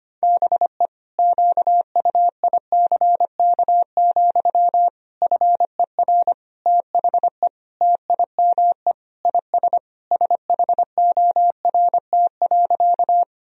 Morse code, heard as a keyed tone, about 25 wpm, 700 Hz. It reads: BE QUICK, FER THE TIME IS SHORT.
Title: Sentences from Top 500 Words and Abbreviations 25wpm